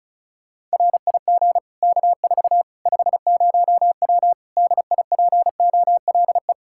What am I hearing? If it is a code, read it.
Morse code, heard as a keyed tone, about 36 wpm, 700 Hz. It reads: RIG K4 50W DIPOLE
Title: Practice QSO - Exchange 2 - Info - 35wpm